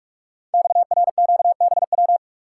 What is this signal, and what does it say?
Morse code, heard as a keyed tone, about 44 wpm, 700 Hz. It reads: XRQBW
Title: Code Groups - 5 Letters - 45wpm